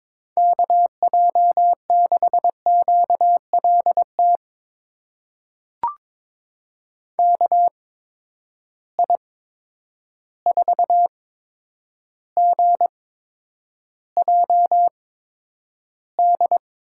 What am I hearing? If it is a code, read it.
Morse code, heard as a keyed tone, about 22 wpm, spaced as for 9 wpm, 700 Hz. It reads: KJ6QLT KI4GJD
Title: USA Call Signs - 6 Characters - 8x Character Spacing - 22wpm